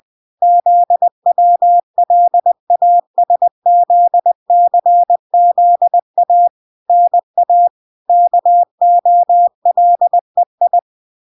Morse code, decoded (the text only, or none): ZWLASZCZA NA KOLEI